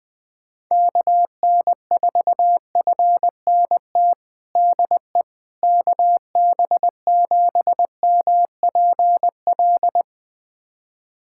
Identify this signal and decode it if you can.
Morse code, heard as a keyed tone, about 20 wpm, 700 Hz. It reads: KN4FNT DE KB7MPL
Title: USA Callsign DE Callsign - 6 Characters - 20wpm